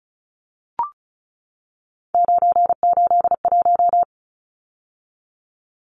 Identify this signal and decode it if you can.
Morse code, heard as a keyed tone, about 34 wpm, 700 Hz. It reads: 981
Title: Numbers - 3-Digit 35wpm